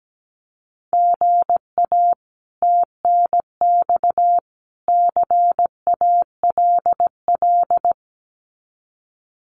Morse code, heard as a keyed tone, about 17 wpm, 700 Hz. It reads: GA TNX CALL
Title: Common QSO Phrases 17wpm